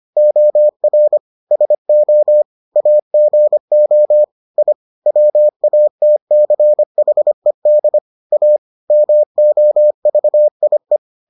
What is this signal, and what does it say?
Morse code, heard as a keyed tone, about 25 wpm, 600 Hz. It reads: OR SO AGO I WATCHED A MOVIE